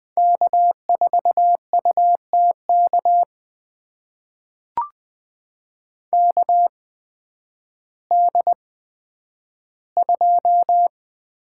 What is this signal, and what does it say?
Morse code, heard as a keyed tone, about 20 wpm, spaced as for 11 wpm, 700 Hz. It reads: K4UTK KD2
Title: USA Call Signs - 5 Characters - 8x Character Spacing - 20wpm